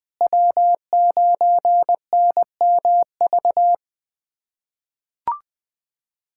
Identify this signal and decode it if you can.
Morse code, heard as a keyed tone, about 20 wpm, 700 Hz. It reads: W9NMV
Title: USA Callsign DE Callsign - 5 Characters - 20wpm